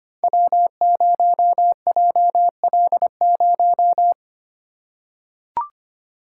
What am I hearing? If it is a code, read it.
Morse code, heard as a keyed tone, about 25 wpm, 700 Hz. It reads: W0JL0
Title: Code Groups - 5 Letter:Numbers - 25wpm